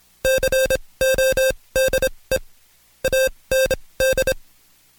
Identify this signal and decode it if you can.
Morse code, heard as a keyed tone, about 26 wpm, spaced as for 20 wpm, 530 Hz. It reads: CODE AND
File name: Code Quick / AudioDownld2 / Seg136-Stepper9 19WPM